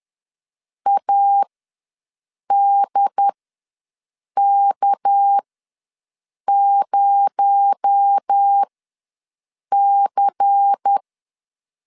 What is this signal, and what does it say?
Morse code, heard as a keyed tone, about 10 wpm, spaced as for 6 wpm, 800 Hz. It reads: ADK0C
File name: LEZIONI / 6 PKX0 / 6 PKX0 ws 10wpm 4220